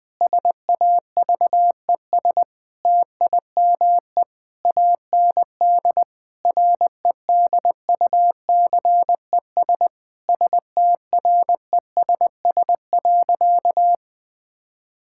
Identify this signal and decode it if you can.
Morse code, heard as a keyed tone, about 20 wpm, 700 Hz. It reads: SAVES TIME AND REDUCES STRESS.
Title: Difficult Sentences with Hints - 20wpm